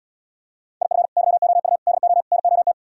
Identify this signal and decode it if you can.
Morse code, heard as a keyed tone, about 77 wpm, 700 Hz. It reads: IF YOU DO NOT